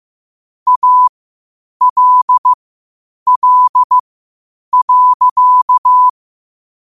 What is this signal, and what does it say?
Morse code, heard as a keyed tone, about 15 wpm, spaced as for 8 wpm, 1000 Hz. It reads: ALL.
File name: qotdfiles / 8wpm